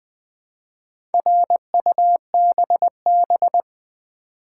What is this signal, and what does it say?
Morse code, heard as a keyed tone, about 20 wpm, 700 Hz. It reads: RUBB
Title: Build A Word - 2 Letters At A Time - 6-Letter Words - 20wpm